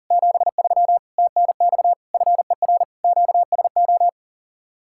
Morse code, heard as a keyed tone, about 49 wpm, 700 Hz. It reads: 73 TNX FER QSO
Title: Speed-Racing Common QSO Phrases - 40wpm